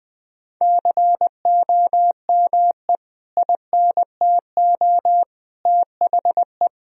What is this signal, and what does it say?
Morse code, heard as a keyed tone, about 20 wpm, 700 Hz. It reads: COME INTO THE
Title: Sentences from Top 1000 Words - Review Part 4 - 20wpm